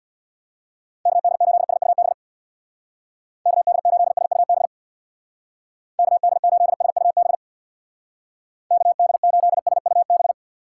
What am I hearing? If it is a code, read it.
Morse code, heard as a keyed tone, about 65 wpm, 700 Hz. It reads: KD8SUB KD8SUB KD8SUB KD8SUB